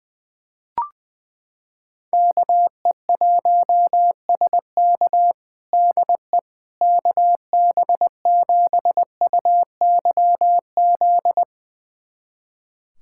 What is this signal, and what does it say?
Morse code, heard as a keyed tone, about 20 wpm, 700 Hz. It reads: KE1SK DE KB7UYZ
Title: USA Callsign DE Callsign 20wpm